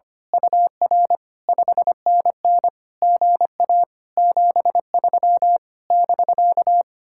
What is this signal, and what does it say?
Morse code, heard as a keyed tone, about 24 wpm, 700 Hz. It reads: UR 5NN GA 73 <BK>
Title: Speed-Racing Common QSO Phrases - 25wpm